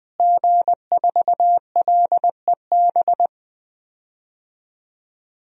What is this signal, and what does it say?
Morse code, heard as a keyed tone, about 20 wpm, 700 Hz. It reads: G4LEB
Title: DXCC-100 Call Signs - 5 Characters - Repeated 3x - 20wpm